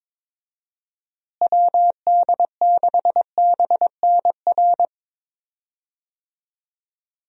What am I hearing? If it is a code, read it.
Morse code, heard as a keyed tone, about 22 wpm, 700 Hz. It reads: WD6BNR
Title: USA Call Signs - 6 Characters - Repeated 2x - 22wpm